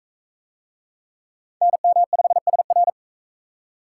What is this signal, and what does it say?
Morse code, heard as a keyed tone, about 41 wpm, 700 Hz. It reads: NM5SR